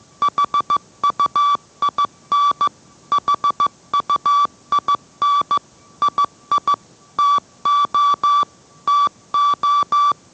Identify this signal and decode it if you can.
Morse code, heard as a keyed tone, about 15 wpm, 1160 Hz. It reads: HUIN HUIN II TO TO